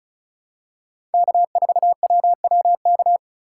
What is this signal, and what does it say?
Morse code, heard as a keyed tone, about 34 wpm, 700 Hz. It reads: K4WWK
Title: USA Call Signs - 5 Characters - 3x Character Spacing - 35wpm